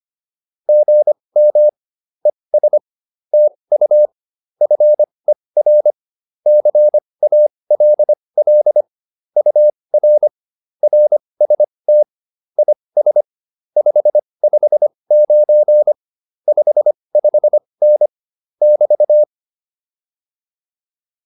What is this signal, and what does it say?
Morse code, heard as a keyed tone, about 25 wpm, spaced as for 20 wpm, 600 Hz. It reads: GM ES TU FER CALL UR RST IS 559 55N <BT>
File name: QSO109a 20